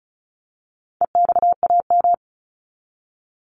Morse code, heard as a keyed tone, about 34 wpm, 700 Hz. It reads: EXAM
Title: Top 501-750 Words 35wpm